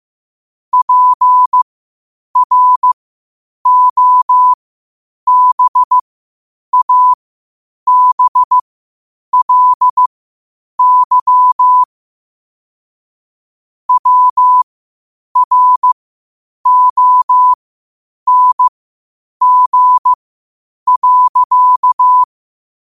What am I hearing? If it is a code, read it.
Morse code, heard as a keyed tone, about 15 wpm, spaced as for 8 wpm, 1000 Hz. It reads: PROBABLY WRONG.
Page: New quotes every day in morse code at 8 Words per minute.